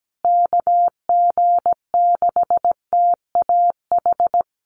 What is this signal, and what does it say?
Morse code, heard as a keyed tone, about 17 wpm, 700 Hz. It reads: KG6TAH
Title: USA Call Signs Repeated 3x 17wpm